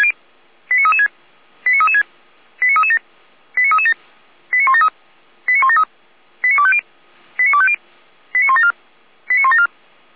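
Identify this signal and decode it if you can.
Five-tone selective call (ZVEI), heard as quick runs of five tones, tones of about 70 ms. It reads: 89336 89336 89337 89337 89173 89173 89250 89250 89174 89174
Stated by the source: ZVEI five-tone selective call 89108 received on an SDR, Feuerwehr Orlamünde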